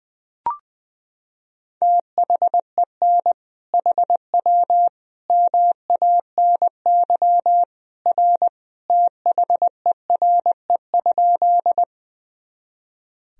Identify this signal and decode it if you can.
Morse code, heard as a keyed tone, about 20 wpm, 700 Hz. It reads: THEN HW MANY R THERE?
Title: Sentences from Top 100 Words and Abbreviations 20wpm